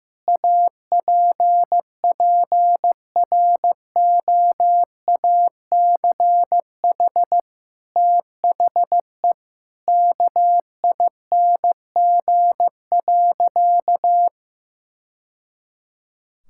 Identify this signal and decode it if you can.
Morse code, heard as a keyed tone, about 15 wpm, 700 Hz. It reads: APPROACH THE KING.
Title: Sentences from Top 1000 Words - Review Part 6 - 15wpm